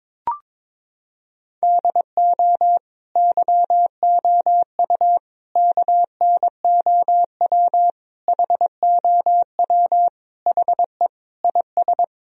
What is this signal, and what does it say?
Morse code, heard as a keyed tone, about 21 wpm, 700 Hz. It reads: DO YOU KNOW HOW HE IS
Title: Sentences from Top 1000 Words - Review Part 3 - 22wpm